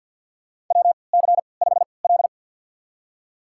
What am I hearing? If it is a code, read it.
Morse code, heard as a keyed tone, about 48 wpm, spaced as for 28 wpm, 700 Hz. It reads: WC5L
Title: USA Call Signs - 4 Characters - 3x Character Spacing - 50wpm